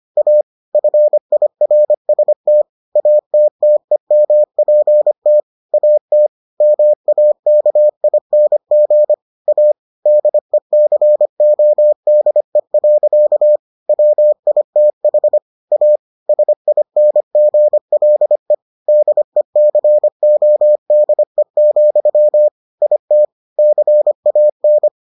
Morse code, heard as a keyed tone, about 25 wpm, 600 Hz. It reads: A FIRST ATTEMPT AT MAKING A DECODE. WITH A SINGLE DECODE, IT CAN